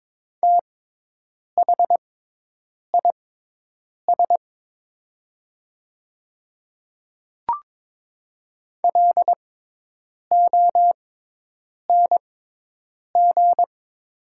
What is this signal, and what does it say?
Morse code, heard as a keyed tone, about 22 wpm, spaced as for 8 wpm, 700 Hz. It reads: THIS LONG
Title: 6x Character Spacing - Top 100 Words - 22wpm